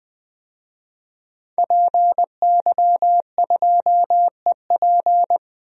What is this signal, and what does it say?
Morse code, heard as a keyed tone, about 20 wpm, 700 Hz. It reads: PY2EP
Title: DXCC-100 Call Signs - 5 Characters - 20wpm